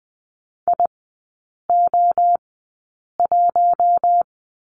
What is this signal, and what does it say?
Morse code, heard as a keyed tone, about 20 wpm, spaced as for 8 wpm, 690 Hz. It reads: IO1
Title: 060 Lesson 11e - Letter h - Three Characters 20wpm